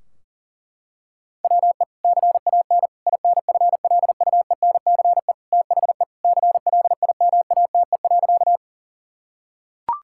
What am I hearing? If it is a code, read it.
Morse code, heard as a keyed tone, about 40 wpm, 700 Hz. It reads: WE CAN INFLUENCE THE CLIMATE.
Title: Sentences from Top 1600 Words 40wpm